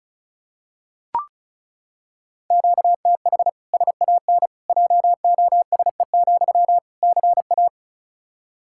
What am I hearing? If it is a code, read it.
Morse code, heard as a keyed tone, about 36 wpm, 700 Hz. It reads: QTH SAN JOSE, CA
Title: Practice QSO - Exchange 1 - Beginning - 35wpm